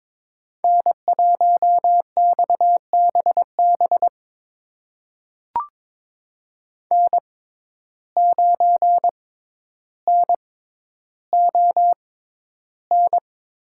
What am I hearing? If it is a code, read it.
Morse code, heard as a keyed tone, about 22 wpm, spaced as for 11 wpm, 700 Hz. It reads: N1XBB N9NON
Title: USA Call Signs - 5 Characters - 6x Character Spacing - 22wpm